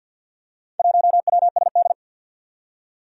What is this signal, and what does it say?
Morse code, heard as a keyed tone, about 47 wpm, 700 Hz. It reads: 1WSD
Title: Four Challenging Letter-Number Combinations 50wpm